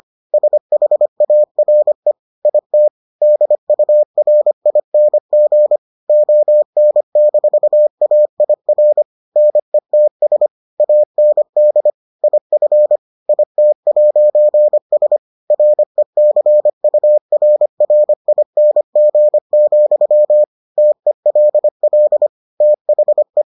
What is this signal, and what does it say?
Morse code, heard as a keyed tone, about 25 wpm, 600 Hz. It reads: SHARE IT DURING ON-AIR NETS AND IF IT'S RECURRING, TELL THE